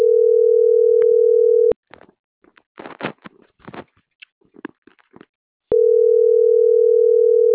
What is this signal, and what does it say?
Ringback tone (North America), 440 + 480 Hz.